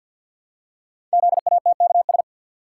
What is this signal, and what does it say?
Morse code, heard as a keyed tone, about 51 wpm, 700 Hz. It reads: ZATKS
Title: Code Groups - 5 Letters - 50wpm